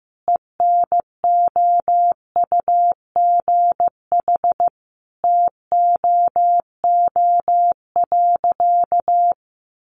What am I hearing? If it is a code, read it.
Morse code, heard as a keyed tone, about 15 wpm, 700 Hz. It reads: ENOUGH TOO.
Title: Easy Phrases 15wpm